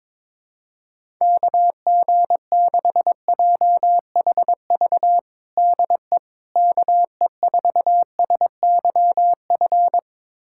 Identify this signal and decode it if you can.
Morse code, heard as a keyed tone, about 21 wpm, 700 Hz. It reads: KG6JHV DE KE4SYF
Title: USA Callsign DE Callsign - 6 Characters - 22wpm